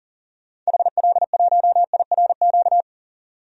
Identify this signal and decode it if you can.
Morse code, heard as a keyed tone, about 40 wpm, 700 Hz. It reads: HL1IRQ
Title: DXCC-100 Call Signs - 6 Characters - 40wpm